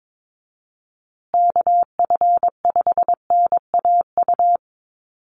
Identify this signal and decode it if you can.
Morse code, heard as a keyed tone, about 22 wpm, 700 Hz. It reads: KF5NAU
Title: USA Call Signs Repeated 2x 22wpm